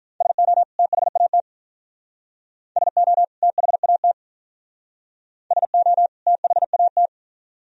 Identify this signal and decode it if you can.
Morse code, heard as a keyed tone, about 47 wpm, 700 Hz. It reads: SO THAT SO THAT SO THAT